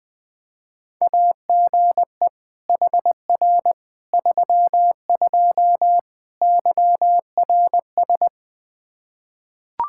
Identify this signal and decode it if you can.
Morse code, heard as a keyed tone, about 21 wpm, 700 Hz. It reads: AGE HR 32 YRS
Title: Practice QSO - Exchange 3 - More Info - 20wpm